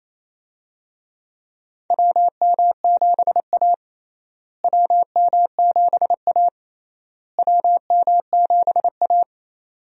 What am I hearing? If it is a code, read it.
Morse code, heard as a keyed tone, about 28 wpm, 700 Hz. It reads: WM7A WM7A WM7A
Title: USA Call Signs - 4 Characters - Repeated 3x - 28wpm